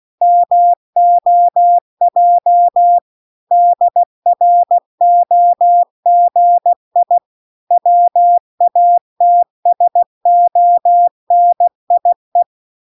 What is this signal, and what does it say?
Morse code, heard as a keyed tone, about 16 wpm, 700 Hz. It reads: MOJ DROGI WATSONIE